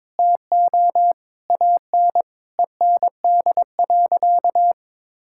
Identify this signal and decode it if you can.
Morse code, heard as a keyed tone, about 22 wpm, 700 Hz. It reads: TO AN END.
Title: Sentences from Top 1000 Words - Review Part 6 - 22wpm